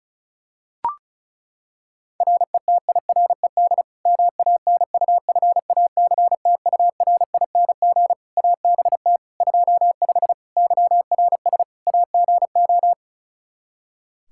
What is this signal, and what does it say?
Morse code, heard as a keyed tone, about 34 wpm, 700 Hz. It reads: RETIRED MANUFACTURING ABT 25 YRS AGO
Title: K1USN SST QSO Practice 35wpm